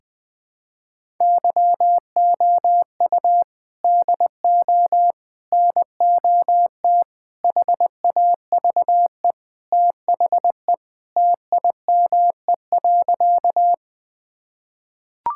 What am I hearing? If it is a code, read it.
Morse code, heard as a keyed tone, about 20 wpm, 700 Hz. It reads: YOU DO NOT HAVE THE TIME.
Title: Sentences with 4-Word Phrases - 20wpm